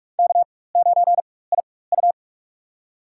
Morse code, heard as a keyed tone, about 43 wpm, spaced as for 21 wpm, 700 Hz. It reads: K9IU
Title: USA Call Signs - 4 Characters - 4x Character Spacing - 45wpm